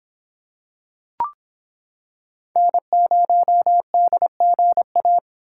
Morse code, heard as a keyed tone, about 26 wpm, 700 Hz. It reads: N0DGA